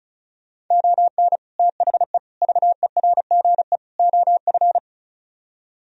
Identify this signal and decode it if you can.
Morse code, heard as a keyed tone, about 42 wpm, 700 Hz. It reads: ON THE VERGE OF
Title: Speed-Racing - Sets of 4 Words 35wpm